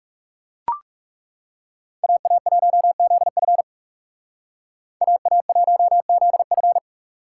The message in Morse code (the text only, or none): AA1ZF AA1ZF